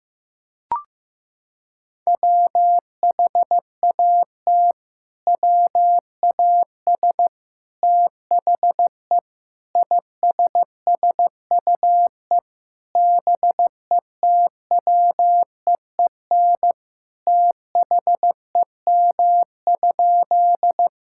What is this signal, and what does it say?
Morse code, heard as a keyed tone, about 15 wpm, 700 Hz. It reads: WHAT WAS THE ISSUE BETWEEN THEM?
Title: Sentences from Top 1000 Words - Review Part 6 - 15wpm